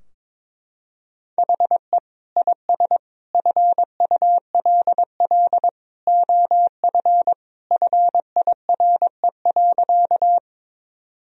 Morse code, heard as a keyed tone, about 22 wpm, 700 Hz. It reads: HE IS FULL OF FIRE.
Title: Sentences from Top 1000 Words - Review Part 3 - 22wpm